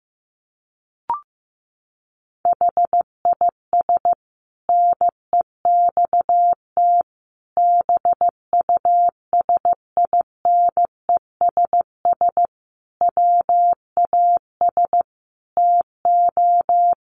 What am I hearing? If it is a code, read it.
Morse code, heard as a keyed tone, about 15 wpm, 700 Hz. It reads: HIS NEXT BUSINESS WAS TO
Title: Sentences from Top 1300 Words 15wpm